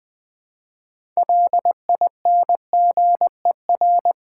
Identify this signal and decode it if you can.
Morse code, heard as a keyed tone, about 20 wpm, 700 Hz. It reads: LINGER
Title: 6 Letter Words - First 3 Letters Given - 20wpm